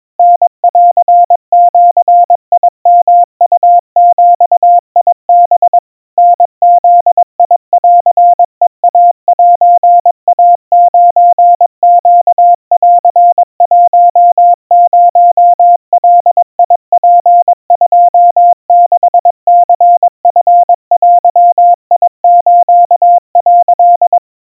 Morse code (Japanese Wabun code), heard as a keyed tone, about 22 wpm, 700 Hz. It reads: タンジヨウビハ タブンヘイセイ9ネン10ガツ26ニチデス」
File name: 80 cpm 700 Hz Crispy